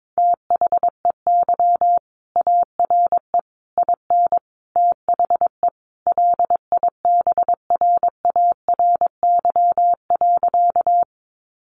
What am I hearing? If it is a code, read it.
Morse code, heard as a keyed tone, about 22 wpm, 700 Hz. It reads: THEY ARE IN THE LIBRARY.
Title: Sentences from Top 600 Words 22wpm